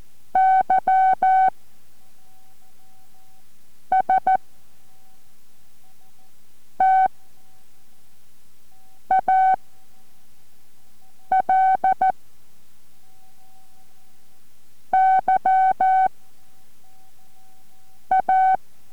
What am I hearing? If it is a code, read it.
Morse code, heard as a keyed tone, about 14 wpm, 760 Hz. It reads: Y S T A L Y A